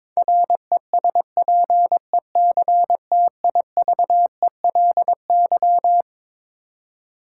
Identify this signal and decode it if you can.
Morse code, heard as a keyed tone, about 21 wpm, 700 Hz. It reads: RESPECTIVELY
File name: Twelve Letter Words - 22wpm